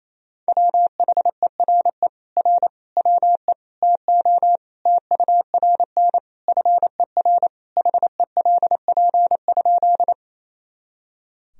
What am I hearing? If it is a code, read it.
Morse code, heard as a keyed tone, about 28 wpm, 700 Hz. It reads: WHERE R WE TO TURN FER HELP?
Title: Sentences from Top 200 Words and Abbreviations 28wpm